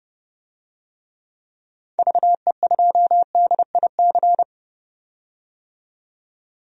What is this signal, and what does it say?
Morse code, heard as a keyed tone, about 30 wpm, 700 Hz. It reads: VE2DIC
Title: DXCC-100 Call Signs - 6 Characters - 30wpm